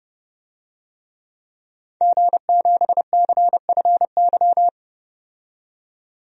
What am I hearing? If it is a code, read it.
Morse code, heard as a keyed tone, about 30 wpm, 700 Hz. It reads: G7CFY
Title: DXCC-100 Call Signs - 5 Characters - 30wpm